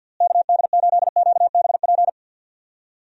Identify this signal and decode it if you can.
Morse code, heard as a keyed tone, about 51 wpm, 700 Hz. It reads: KD8QBP
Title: USA Call Signs - 6 Characters - Repeated 3x - 50wpm